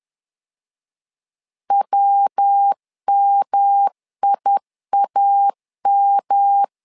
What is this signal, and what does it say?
Morse code, heard as a keyed tone, about 10 wpm, 800 Hz. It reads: WMIAM
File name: LEZIONI / 3 MWRO / 3 MW 10wpm 1959